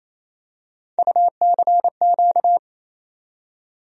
Morse code, heard as a keyed tone, about 28 wpm, 700 Hz. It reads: UCQ
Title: USA Call Sign Postfixes - 3 Letters 28wpm